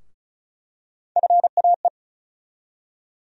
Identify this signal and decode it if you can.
Morse code, heard as a keyed tone, about 36 wpm, 700 Hz. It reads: FAE
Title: USA Call Sign Postfixes - 3 Letters 35wpm